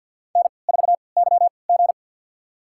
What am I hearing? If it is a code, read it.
Morse code, heard as a keyed tone, about 53 wpm, spaced as for 29 wpm, 700 Hz. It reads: N4YG
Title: USA Call Signs - 4 Characters - 3x Character Spacing - 50wpm